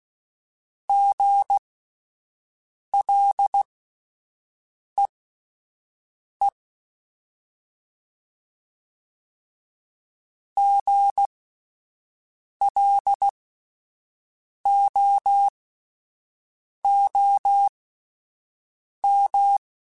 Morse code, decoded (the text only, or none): GLEE GLOOM